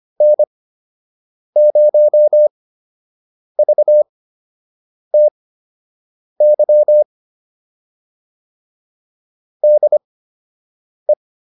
Morse code, heard as a keyed tone, about 25 wpm, spaced as for 7 wpm, 600 Hz. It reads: N0VTY DE